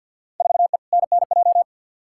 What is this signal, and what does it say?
Morse code, heard as a keyed tone, about 51 wpm, 700 Hz. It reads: 4E NNJ
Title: Field Day Exchanges - 50wpm